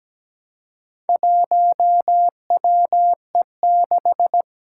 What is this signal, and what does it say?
Morse code, heard as a keyed tone, about 17 wpm, 700 Hz. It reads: 1WE6
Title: Four Challenging Letter-Number Combinations 17wpm